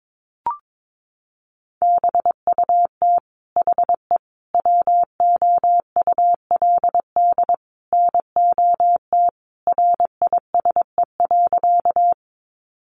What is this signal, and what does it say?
Morse code, heard as a keyed tone, about 22 wpm, 700 Hz. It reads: BUT HE WOULD NOT RISE.
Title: Sentences from Top 900 Words 22wpm